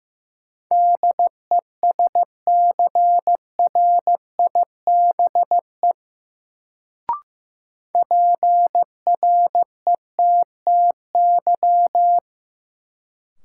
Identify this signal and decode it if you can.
Morse code, heard as a keyed tone, about 15 wpm, 700 Hz. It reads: DESCRIBE PRETTY
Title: Sentences from Top 1000 Words - New Word Drill - 15wpm